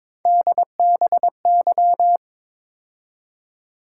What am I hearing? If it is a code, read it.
Morse code, heard as a keyed tone, about 26 wpm, 700 Hz. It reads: DBY
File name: Speed-Racing - Three Letter-Number Combinations - 22wpm